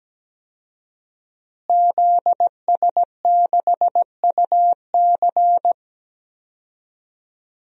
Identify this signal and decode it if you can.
Morse code, heard as a keyed tone, about 17 wpm, 700 Hz. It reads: ZS6UC
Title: DXCC-100 Call Signs - 5 Characters - Repeated 2x - 17wpm